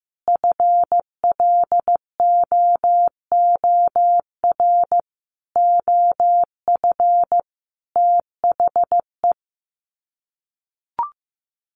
Morse code, heard as a keyed tone, about 15 wpm, 700 Hz. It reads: FLOOR OF THE
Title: Sets of 5 Words 15wpm